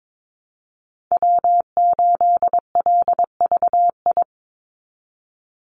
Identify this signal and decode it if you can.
Morse code, heard as a keyed tone, about 22 wpm, 700 Hz. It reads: W8LVI
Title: Call Signs Repeated 2x - 22wpm